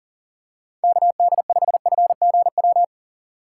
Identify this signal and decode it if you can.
Morse code, heard as a keyed tone, about 39 wpm, 700 Hz. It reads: KD5FGW
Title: USA Call Signs - 8x Character Spacing - 40wpm